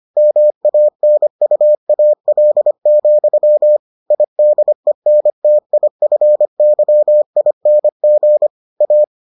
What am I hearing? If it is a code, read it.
Morse code, heard as a keyed tone, about 25 wpm, 600 Hz. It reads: MANUAL, IDENTIFYING A